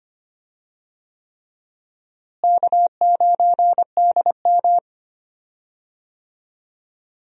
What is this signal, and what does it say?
Morse code, heard as a keyed tone, about 25 wpm, 700 Hz. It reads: K9DM
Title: USA Call Signs - 4 Characters - Repeated 3x - 25wpm